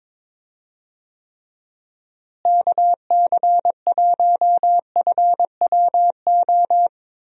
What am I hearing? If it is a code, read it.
Morse code, heard as a keyed tone, about 22 wpm, 700 Hz. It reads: KC1FWO
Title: USA Call Signs - 6 Characters - Repeated 3x - 22wpm